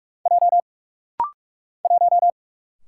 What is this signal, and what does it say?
Morse code, heard as a keyed tone, about 45 wpm, 700 Hz. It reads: J 1
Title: Single Letter-Number Alt - Rapid-Fire 45wpm